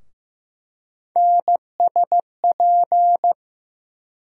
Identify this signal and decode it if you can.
Morse code, heard as a keyed tone, about 15 wpm, 700 Hz. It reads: NSP
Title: USA Call Sign Postfixes - 3 Letters 15wpm